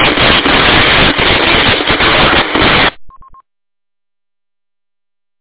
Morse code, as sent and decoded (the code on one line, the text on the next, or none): none